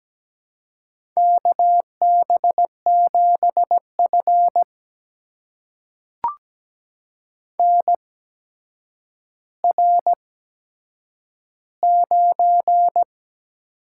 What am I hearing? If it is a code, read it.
Morse code, heard as a keyed tone, about 17 wpm, spaced as for 8 wpm, 700 Hz. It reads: KB7F NR9
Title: USA Call Signs - 4 Characters - 8x Character Spacing - 17wpm